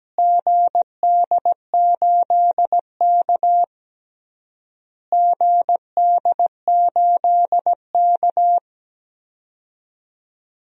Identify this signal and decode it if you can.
Morse code, heard as a keyed tone, about 17 wpm, 700 Hz. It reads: GD8K GD8K
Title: DXCC-100 Call Signs - 4 Characters - Repeated 3x - 17wpm